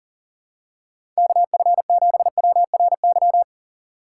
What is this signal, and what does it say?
Morse code, heard as a keyed tone, about 40 wpm, 700 Hz. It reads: KF7WRY
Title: USA Call Signs - 6 Characters - 4x Character Spacing - 40wpm